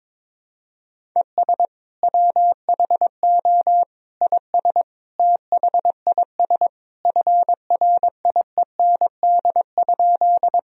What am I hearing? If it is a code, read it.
Morse code, heard as a keyed tone, about 22 wpm, 700 Hz. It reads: ES WHO IS THIS FRIEND?
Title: Sentences from Top 300 Words and Abbreviations 22wpm